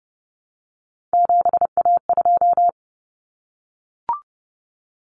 Morse code, heard as a keyed tone, about 30 wpm, 700 Hz. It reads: 7A2
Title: Common Cut Numbers - 3-Digit 30wpm